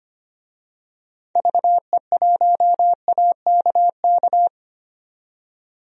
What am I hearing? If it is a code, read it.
Morse code, heard as a keyed tone, about 25 wpm, 700 Hz. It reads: VE1AKK
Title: DXCC-100 Call Signs - 6 Characters - 25wpm